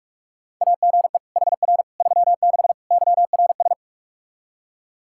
Morse code, heard as a keyed tone, about 44 wpm, 700 Hz. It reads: AGE HR 36 YRS
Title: Practice QSO - Exchange 3 - More Info - 45wpm